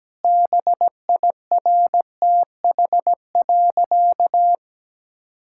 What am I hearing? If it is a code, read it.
Morse code, heard as a keyed tone, about 17 wpm, 700 Hz. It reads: BIRTH.
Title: Sentences from Top 1000 Words - Review Part 5 - 17wpm